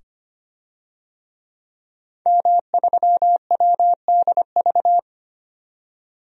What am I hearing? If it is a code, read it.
Morse code, heard as a keyed tone, about 25 wpm, 700 Hz. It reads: M3WDV
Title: DXCC-100 Call Signs - 5 Characters - 25wpm